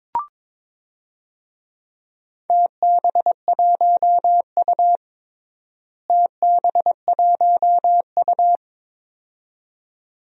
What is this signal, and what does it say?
Morse code, heard as a keyed tone, about 22 wpm, 700 Hz. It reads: TB1U TB1U
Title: DXCC-100 Call Signs - 4 Characters - Repeated 2x - 22wpm